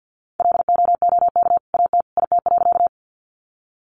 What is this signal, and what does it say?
Morse code, heard as a keyed tone, about 50 wpm, 700 Hz. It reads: LOOK AT IT.
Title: Sentences from Top 100 Words 50wpm